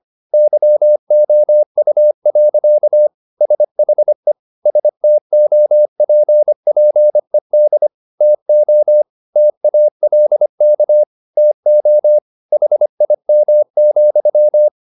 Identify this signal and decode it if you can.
Morse code, heard as a keyed tone, about 25 wpm, 600 Hz. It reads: YOU. SHE STOPPED TO TALK TO HIM,